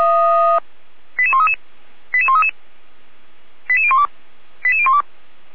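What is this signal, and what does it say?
Five-tone selective call (ZVEI), heard as quick runs of five tones, tones of about 70 ms. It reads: 88133 88133 88012 88012
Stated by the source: Five-tone ZVEI selective call 88199 received by SDR, logged by fire brigade Feuerwehr Orlamünde.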